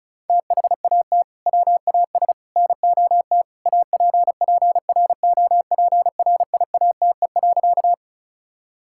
Morse code, heard as a keyed tone, about 36 wpm, 700 Hz. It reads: THAT WAS NOT APPROPRIATE.
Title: Sentences from Top 1000 Words - Review Part 5 - 35wpm